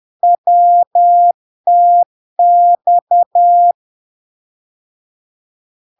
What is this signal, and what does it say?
Morse code, heard as a keyed tone, about 10 wpm, 700 Hz. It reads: WTX
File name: LARK-FD-2022-Log-in-CW-10wpm-700-Hz